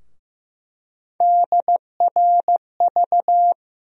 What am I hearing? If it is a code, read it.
Morse code, heard as a keyed tone, about 15 wpm, 700 Hz. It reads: DRV
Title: USA Call Sign Postfixes - 3 Letters 15wpm